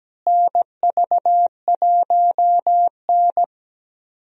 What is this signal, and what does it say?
Morse code, heard as a keyed tone, about 17 wpm, 700 Hz. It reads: NV1N
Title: USA Call Signs - 4 Characters - Repeated 3x - 17wpm